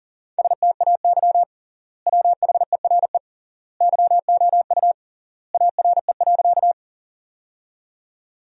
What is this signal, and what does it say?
Morse code, heard as a keyed tone, about 40 wpm, 700 Hz. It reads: STAY WHERE YOU ARE.
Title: 3x Word-Spacing - Sentences from Top 500 Words - 40wpm